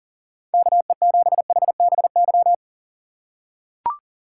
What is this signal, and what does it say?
Morse code, heard as a keyed tone, about 40 wpm, 700 Hz. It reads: KE7HBY
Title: USA Call Signs - 6 Characters - 4x Character Spacing - 40wpm